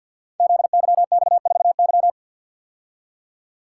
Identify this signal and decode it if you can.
Morse code, heard as a keyed tone, about 51 wpm, 700 Hz. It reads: ZYX4Y
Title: Code Groups - 5 Letter:Numbers - 50wpm